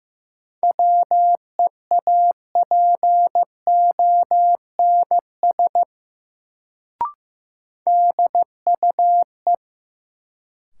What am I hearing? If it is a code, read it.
Morse code, heard as a keyed tone, about 15 wpm, 700 Hz. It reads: WEAPONS DUE
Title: Sets of 5 Words Encore - New Word Drill - 15wpm